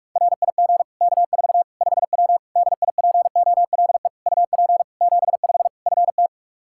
Morse code, heard as a keyed tone, about 46 wpm, 700 Hz. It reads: RIG K4 5W DIPOLE UP 75 FT
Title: Practice QSO - Exchange 2 - Info - 45wpm